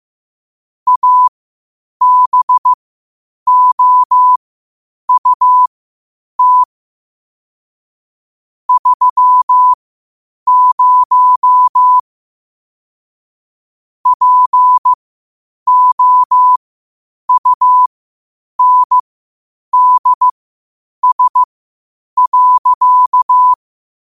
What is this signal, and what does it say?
Morse code, heard as a keyed tone, about 15 wpm, spaced as for 8 wpm, 1000 Hz. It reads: ABOUT 30 POUNDS.
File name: qotdfiles / 8wpm